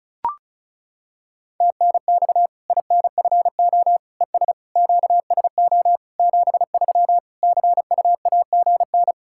Morse code, heard as a keyed tone, about 34 wpm, 700 Hz. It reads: TNX INFO ES QSO 73 CUAGN
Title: K1USN SST QSO Practice 35wpm